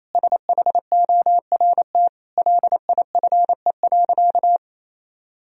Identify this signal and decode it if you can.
Morse code, heard as a keyed tone, about 28 wpm, 700 Hz. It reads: SHORT LIFE.
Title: Sentences from Top 1000 Words - Review Part 2 - 28wpm